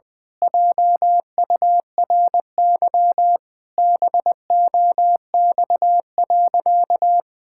Morse code, heard as a keyed tone, about 21 wpm, 700 Hz. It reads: JURY BOX.
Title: Pangram Challenge v2 - 20wpm